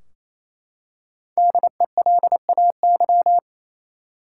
Morse code, heard as a keyed tone, about 28 wpm, 700 Hz. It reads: DELAY
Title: Five Letter Words - 28wpm